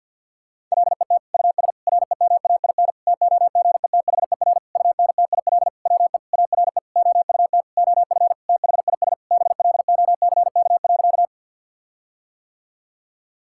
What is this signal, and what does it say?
Morse code, heard as a keyed tone, about 50 wpm, 700 Hz. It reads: LET US REMAIN TOGETHER UNTIL WE ARE OUT OF THIS BLOCK.
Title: Sentences from Top 1600 Words 50wpm